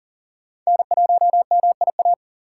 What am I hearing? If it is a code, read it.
Morse code, heard as a keyed tone, about 40 wpm, 700 Hz. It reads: N1MIA
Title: USA Call Signs - 3x Character Spacing - 40wpm